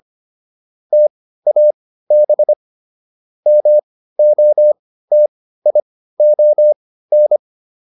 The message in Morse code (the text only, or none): TAB MOTION